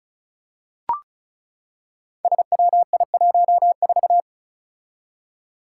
Morse code, heard as a keyed tone, about 36 wpm, 700 Hz. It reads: SWI14
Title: Five Challenging Letter-Number Combinations 35wpm